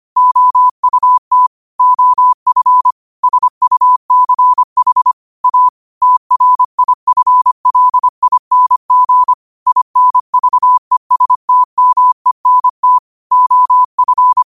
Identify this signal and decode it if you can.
Morse code, heard as a keyed tone, about 25 wpm, 1000 Hz. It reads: OUT OF SUCH A TRIFLING INVESTMENT OF